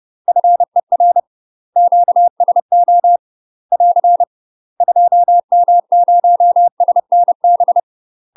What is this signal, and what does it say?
Morse code, heard as a keyed tone, about 30 wpm, 700 Hz. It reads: FER QSO <AR> 2M0SNB
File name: UK MASTER QSOs TRAINING/30WPM